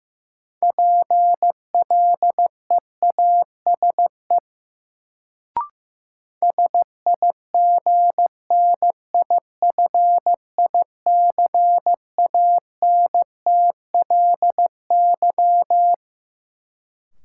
Morse code, heard as a keyed tone, about 15 wpm, 700 Hz. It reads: PLEASE SIGNIFICANTLY
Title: Sentences from Top 1500 Words - New Word Drill - 15wpm